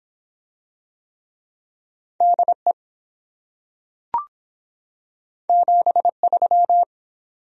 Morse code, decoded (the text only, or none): DE 73